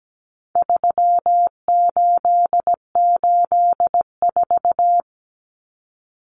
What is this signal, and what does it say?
Morse code, heard as a keyed tone, about 17 wpm, 700 Hz. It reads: 3884
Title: Common Cut Numbers - 4-Digit 17wpm